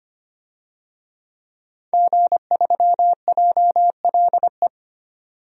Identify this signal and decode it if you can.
Morse code, heard as a keyed tone, about 25 wpm, 700 Hz. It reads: G3JLE
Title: DXCC-100 Call Signs - 5 Characters - 25wpm